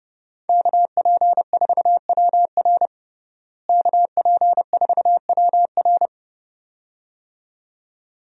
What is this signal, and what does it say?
Morse code, heard as a keyed tone, about 30 wpm, 700 Hz. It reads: KP4WR KP4WR
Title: USA Call Signs - 5 Characters - Repeated 2x - 30wpm